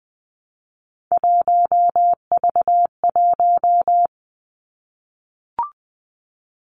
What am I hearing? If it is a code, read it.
Morse code, heard as a keyed tone, about 20 wpm, 700 Hz. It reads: 1V1
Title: Three Challenging Letter-Number Combinations 20wpm